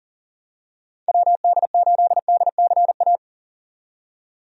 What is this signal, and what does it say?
Morse code, heard as a keyed tone, about 40 wpm, 700 Hz. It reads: WD8DCA
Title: USA Call Signs - 6 Characters - 40wpm